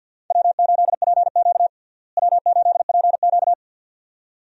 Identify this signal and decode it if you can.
Morse code, heard as a keyed tone, about 47 wpm, 700 Hz. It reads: W8PQ W8PQ
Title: USA Call Signs - 4 Characters - Repeated 2x - 50wpm